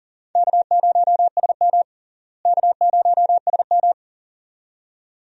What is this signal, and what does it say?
Morse code, heard as a keyed tone, about 37 wpm, 700 Hz. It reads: K0SM K0SM
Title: USA Call Signs - 4 Characters - Repeated 2x - 40wpm